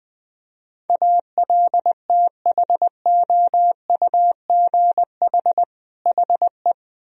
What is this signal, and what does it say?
Morse code, heard as a keyed tone, about 20 wpm, 700 Hz. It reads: ALTHOUGH HE
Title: Sentences from Top 1000 Words - Review Part 3 - 20wpm